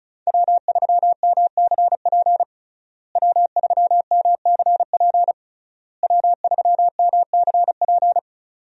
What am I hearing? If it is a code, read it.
Morse code, heard as a keyed tone, about 34 wpm, 700 Hz. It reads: W3MCP W3MCP W3MCP
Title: USA Call Signs - 5 Characters - Repeated 3x - 35wpm